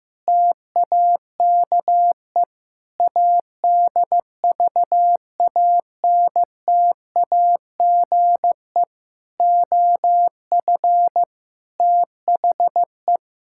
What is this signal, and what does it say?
Morse code, heard as a keyed tone, about 15 wpm, 700 Hz. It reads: TAKE ADVANTAGE OF THE
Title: Sentences from Top 1000 Words - Review Part 6 - 15wpm